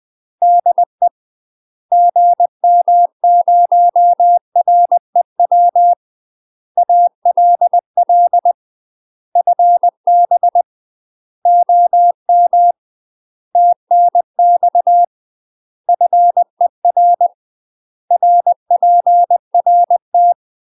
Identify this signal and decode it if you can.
Morse code, heard as a keyed tone, about 20 wpm, 700 Hz. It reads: DE GM0REW ALL FB OM TNX FER RPRT
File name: UK MASTER QSOs TRAINING / 20WPM / UK MASTER QSOs 0012